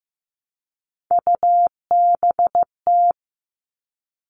Morse code, heard as a keyed tone, about 15 wpm, 700 Hz. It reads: UBT
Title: All Cut Numbers - 3-Digit 15wpm